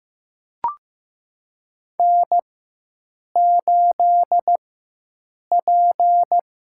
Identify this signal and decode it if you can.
Morse code, heard as a keyed tone, about 16 wpm, spaced as for 7 wpm, 700 Hz. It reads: N8P
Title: USA Call Signs - 5 Characters - 4x Character Spacing - 15wpm